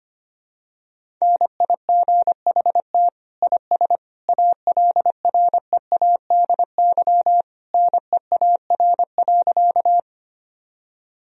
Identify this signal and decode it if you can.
Morse code, heard as a keyed tone, about 25 wpm, 700 Hz. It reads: NIGHT IS ALREADY NEAR.
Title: Sentences from Top 1000 Words - Review Part 1 - 25wpm